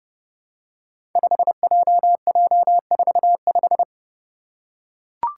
Morse code, heard as a keyed tone, about 30 wpm, 700 Hz. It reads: 5JJ45
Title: Five Challenging Letter-Number Combinations 30wpm